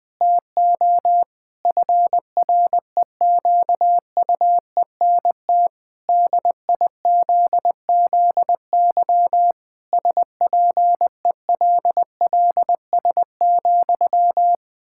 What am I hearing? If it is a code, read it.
Morse code, heard as a keyed tone, about 20 wpm, 700 Hz. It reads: TO FREQUENT DIZZY SPELLS,
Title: Pangram Challenge v2 - 20wpm